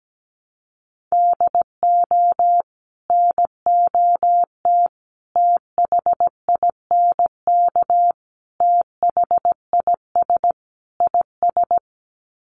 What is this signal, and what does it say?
Morse code, heard as a keyed tone, about 17 wpm, 700 Hz. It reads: DO NOT THINK THIS IS
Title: Sets of 5 Words Encore2 17wpm